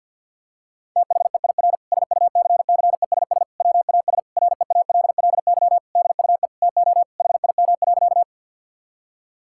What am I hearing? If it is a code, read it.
Morse code, heard as a keyed tone, about 48 wpm, 700 Hz. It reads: THEIR SUCCESS WAS REALLY DUE TO HIM.
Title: Sentences from Top 1000 Words - Review Part 6 - 50wpm